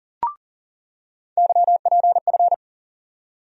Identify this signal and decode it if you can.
Morse code, heard as a keyed tone, about 40 wpm, 700 Hz. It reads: YPF
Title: USA Call Sign Postfixes - 3 Letters 40wpm